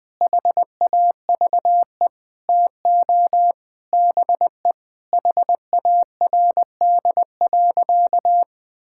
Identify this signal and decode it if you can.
Morse code, heard as a keyed tone, about 20 wpm, 700 Hz. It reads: HAVE TO BE HARD.
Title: Sentences with 4-Word Phrases - 20wpm